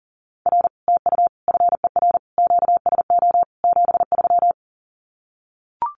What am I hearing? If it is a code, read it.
Morse code, heard as a keyed tone, about 37 wpm, 700 Hz. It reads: R TU FER QSO 73
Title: Common QSO Phrases Encore 40wpm